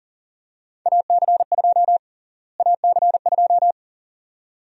Morse code, heard as a keyed tone, about 40 wpm, 700 Hz. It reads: AC2 AC2
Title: USA Call Sign Prefixes - Repeated 2x - 40wpm